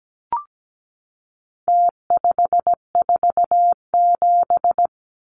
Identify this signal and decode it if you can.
Morse code, heard as a keyed tone, about 17 wpm, 700 Hz. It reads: T547
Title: Numbers - Contest Style - 4-Digit 17wpm